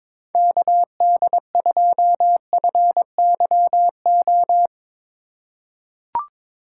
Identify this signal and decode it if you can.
Morse code, heard as a keyed tone, about 22 wpm, 700 Hz. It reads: KD2FYO
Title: USA Call Signs - 6 Characters - 2x Character Spacing - 22wpm